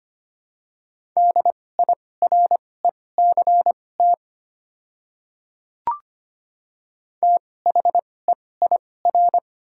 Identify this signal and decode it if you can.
Morse code, heard as a keyed tone, about 25 wpm, spaced as for 18 wpm, 700 Hz. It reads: DIRECT THEIR
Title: 2x Character Spacing - Top 500 Words - 25wpm